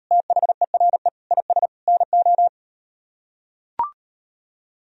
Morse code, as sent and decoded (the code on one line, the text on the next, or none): - .... .-. / .. ... / -. ---
THERE IS NO